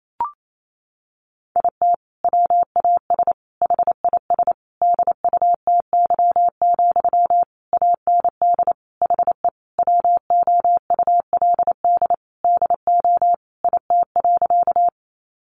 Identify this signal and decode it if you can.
Morse code, heard as a keyed tone, about 28 wpm, 700 Hz. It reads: IT WAS HIS DUTY, AND HE WOULD DO IT.
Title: Sentences from Top 1500 Words 28wpm